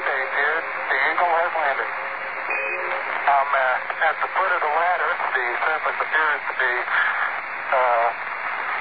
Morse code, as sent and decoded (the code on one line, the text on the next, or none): none